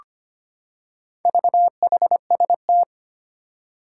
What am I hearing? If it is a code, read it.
Morse code, heard as a keyed tone, about 25 wpm, 700 Hz. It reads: VHST
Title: Four Challenging Letter-Number Combinations 25wpm